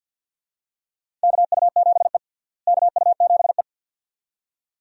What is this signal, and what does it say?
Morse code, heard as a keyed tone, about 52 wpm, 700 Hz. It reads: KU7E KU7E
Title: USA Call Signs - 4 Characters - Repeated 2x - 50wpm